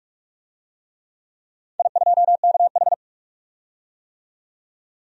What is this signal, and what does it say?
Morse code, heard as a keyed tone, about 46 wpm, 700 Hz. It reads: I2KH
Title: DXCC-100 Call Signs - 4 Characters - 45wpm